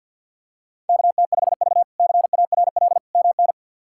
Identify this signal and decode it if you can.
Morse code, heard as a keyed tone, about 49 wpm, 700 Hz. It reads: KT5V CARL MN
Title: K1USN SST QSO Practice - 50wpm